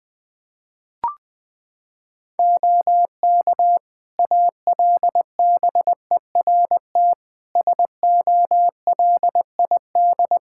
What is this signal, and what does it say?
Morse code, heard as a keyed tone, about 20 wpm, 700 Hz. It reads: OK ALBERT SOLID
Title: Practice QSO - Exchange 3 - More Info - 20wpm